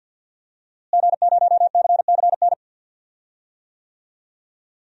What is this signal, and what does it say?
Morse code, heard as a keyed tone, about 51 wpm, 700 Hz. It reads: G0CCN
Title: DXCC-100 Call Signs - 5 Characters - Repeated 3x - 50wpm